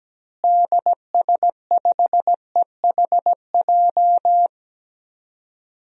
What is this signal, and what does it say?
Morse code, heard as a keyed tone, about 17 wpm, 700 Hz. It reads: DS5EHJ
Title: DXCC-100 Call Signs - 6 Characters - Repeated 2x - 17wpm